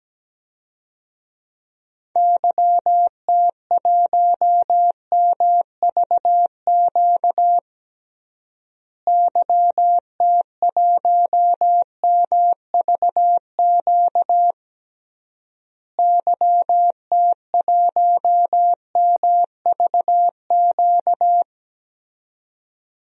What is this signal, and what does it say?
Morse code, heard as a keyed tone, about 17 wpm, 700 Hz. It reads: YT1MVQ YT1MVQ YT1MVQ
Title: DXCC-100 Call Signs - 6 Characters - Repeated 3x - 17wpm